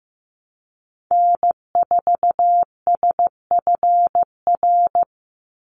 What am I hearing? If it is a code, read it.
Morse code, heard as a keyed tone, about 15 wpm, 700 Hz. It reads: N4SFR
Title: NA SOTA Activator Callsigns 15wpm